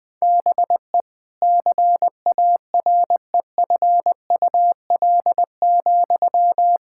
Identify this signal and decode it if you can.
Morse code, heard as a keyed tone, about 21 wpm, 700 Hz. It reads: BE CAREFUL,
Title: Sentences from Top 1600 Words 20wpm